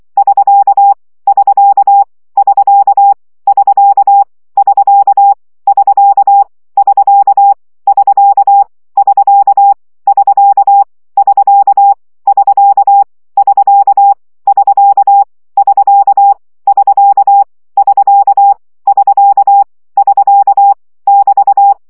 Morse code, heard as a keyed tone, about 24 wpm, 800 Hz. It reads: <SK> <SK> <SK> <SK> <SK> <SK> <SK> <SK> <SK> <SK> <SK> <SK> <SK> <SK> <SK> <SK> <SK> <SK> <SK> <BT>
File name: CW gyakorlas sk 20 baud